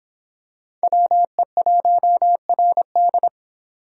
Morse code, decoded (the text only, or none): WE1RD